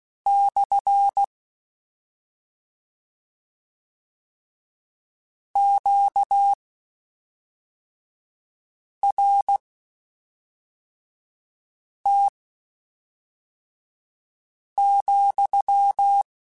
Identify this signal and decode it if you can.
Morse code, heard as a keyed tone, about 16 wpm, spaced as for 3 wpm, 800 Hz. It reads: / QRT,